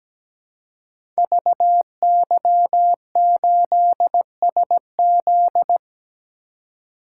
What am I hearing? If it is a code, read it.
Morse code, heard as a keyed tone, about 17 wpm, 700 Hz. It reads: VY8SZ
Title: Code Groups - 5 Letter:Numbers - 17wpm